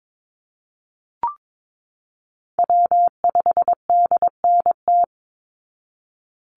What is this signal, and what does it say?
Morse code, heard as a keyed tone, about 22 wpm, 700 Hz. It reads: W5DNT
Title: Call Signs Repeated 2x - 22wpm